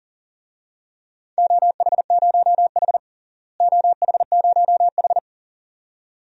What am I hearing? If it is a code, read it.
Morse code, heard as a keyed tone, about 40 wpm, 700 Hz. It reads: OH0H OH0H
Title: DXCC-100 Call Signs - 4 Characters - Repeated 2x - 40wpm